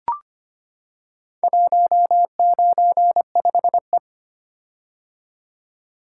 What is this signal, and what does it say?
Morse code, heard as a keyed tone, about 25 wpm, 700 Hz. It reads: 195E
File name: practice / 25wpm / Four Letter-Number Combinations - 25wpm